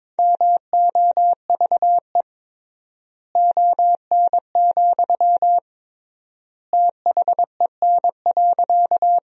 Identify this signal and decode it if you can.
Morse code, heard as a keyed tone, about 22 wpm, 700 Hz. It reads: MOVE ON, THEN.
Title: 3x Word-Spacing - Sentences from Top 200 Words - 22wpm